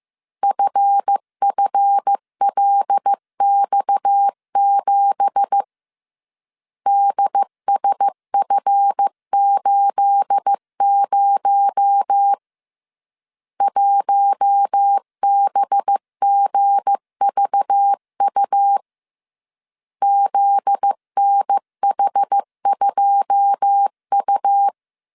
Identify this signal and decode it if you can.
Morse code, heard as a keyed tone, about 15 wpm, 800 Hz. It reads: FFLX7 DSF80 1BGVU ZNH2U
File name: CW / DET / 1 9 1500C S A 14wpm 3222